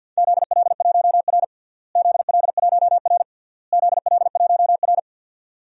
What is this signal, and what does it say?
Morse code, heard as a keyed tone, about 52 wpm, 700 Hz. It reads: ZL1R ZL1R ZL1R
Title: DXCC-100 Call Signs - 4 Characters - Repeated 3x - 50wpm